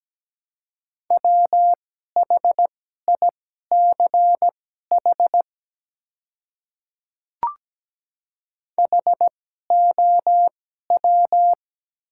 Morse code, heard as a keyed tone, about 17 wpm, spaced as for 12 wpm, 700 Hz. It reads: WHICH HOW
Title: 2x Character Spacing - Top 100 Words - 17wpm